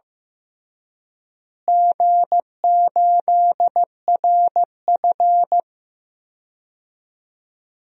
Morse code, heard as a keyed tone, about 16 wpm, 700 Hz. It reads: G8RF
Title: DXCC-100 Call Signs - 4 Characters - Repeated 3x - 15wpm